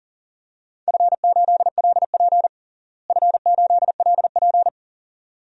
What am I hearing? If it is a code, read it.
Morse code, heard as a keyed tone, about 40 wpm, 700 Hz. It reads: F8LP F8LP
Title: DXCC-100 Call Signs - 4 Characters - Repeated 2x - 40wpm